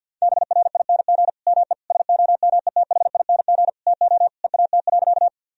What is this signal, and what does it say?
Morse code, heard as a keyed tone, about 50 wpm, 700 Hz. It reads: BRING ME SOMETHING TO EAT.
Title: Sentences from Top 1000 Words - Review Part 3 - 50wpm